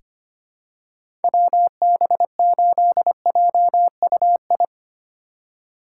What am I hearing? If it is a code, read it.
Morse code, heard as a keyed tone, about 25 wpm, 700 Hz. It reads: WB8JUI
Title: K1USN SST QSO Practice - 25wpm